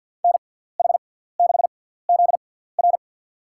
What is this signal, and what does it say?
Morse code, heard as a keyed tone, about 46 wpm, spaced as for 17 wpm, 700 Hz. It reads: NH6ZR
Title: USA Call Signs - 5 Characters - 6x Character Spacing - 50wpm